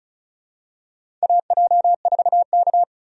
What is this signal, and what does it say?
Morse code, heard as a keyed tone, about 36 wpm, 700 Hz. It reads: AJ4K
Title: USA Call Signs - 4 Characters - 35wpm